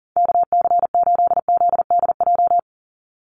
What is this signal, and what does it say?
Morse code, heard as a keyed tone, about 40 wpm, 690 Hz. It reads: KC8ZDJ
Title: USA Call Signs 40wpm